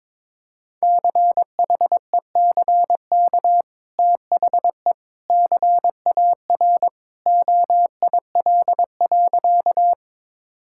Morse code, heard as a keyed tone, about 22 wpm, 700 Hz. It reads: CHECK THE CAR OIL.
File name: practice / 22wpm / Easy Sentences with Hints - 22wpm